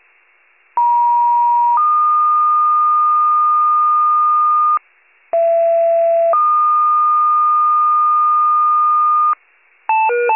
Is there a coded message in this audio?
Yes: two-tone page from 0.8 to 9.3 s, heard as a one-second tone, then a three-second tone, about 950 then 1220 Hz, about 670 then 1220 Hz.